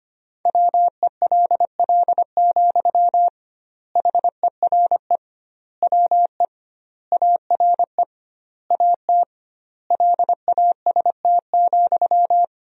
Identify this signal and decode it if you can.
Morse code, heard as a keyed tone, about 25 wpm, 700 Hz. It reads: WELL, HERE WE ARE AT LAST,
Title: 2x Word-Spacing - Sentences from Top 300 Words - 25wpm